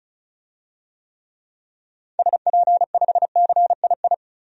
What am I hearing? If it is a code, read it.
Morse code, heard as a keyed tone, about 36 wpm, 700 Hz. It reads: SP5CII
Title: DXCC-100 Call Signs - 6 Characters - 35wpm